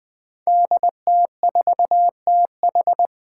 Morse code, heard as a keyed tone, about 20 wpm, 700 Hz. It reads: DT4TH
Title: Five Challenging Letter-Number Combinations 20wpm